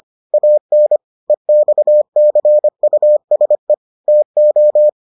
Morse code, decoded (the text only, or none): AN EXCUSE TO